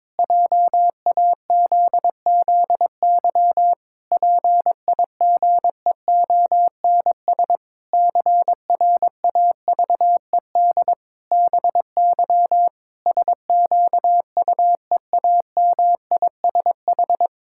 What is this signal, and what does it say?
Morse code, heard as a keyed tone, about 21 wpm, 700 Hz. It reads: JAZZY PIGEONS CRAVED BY SQUEAMISH
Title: Pangram Challenge v2 - 22wpm